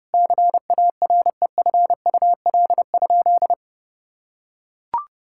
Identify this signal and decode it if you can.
Morse code, heard as a keyed tone, about 30 wpm, 700 Hz. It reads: CAREFUL?
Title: Sentences from Top 1600 Words 30wpm